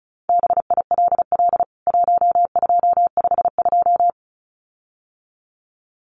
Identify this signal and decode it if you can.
Morse code, heard as a keyed tone, about 34 wpm, 700 Hz. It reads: BILL 1252
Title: CWT Exchange 35wpm